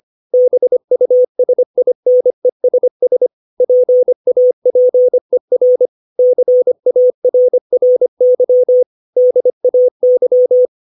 Morse code, heard as a keyed tone, about 25 wpm, 500 Hz. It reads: BUSINESS PAPER CARRY DAY
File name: 632 words / 632 Words 25WPM